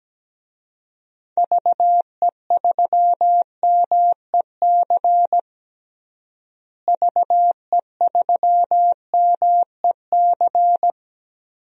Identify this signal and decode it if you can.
Morse code, heard as a keyed tone, about 17 wpm, 700 Hz. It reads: VE3MEC VE3MEC
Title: DXCC-100 Call Signs - 6 Characters - Repeated 3x - 17wpm